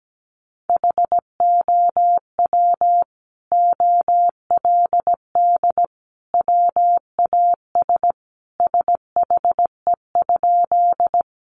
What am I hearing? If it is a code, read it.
Morse code, heard as a keyed tone, about 17 wpm, 700 Hz. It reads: HOW OLD WAS SHE?
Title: Sentences from Top 200 Words 17wpm